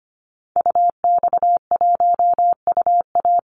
Morse code, heard as a keyed tone, about 25 wpm, 700 Hz. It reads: UX1UA
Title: Call Signs Repeated 3x - 25wpm